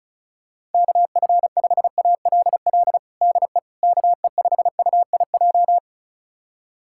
Morse code, heard as a keyed tone, about 36 wpm, 700 Hz. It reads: KF5ALL DE KE5UIJ
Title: USA Callsign DE Callsign 35wpm